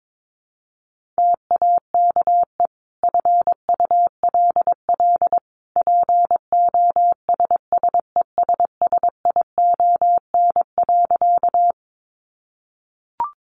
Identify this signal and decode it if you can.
Morse code, heard as a keyed tone, about 22 wpm, 700 Hz. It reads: TAKE FULL POSSESSION.
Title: Sentences from Top 1400 Words 22wpm